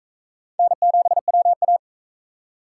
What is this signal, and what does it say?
Morse code, heard as a keyed tone, about 40 wpm, 700 Hz. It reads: N7WA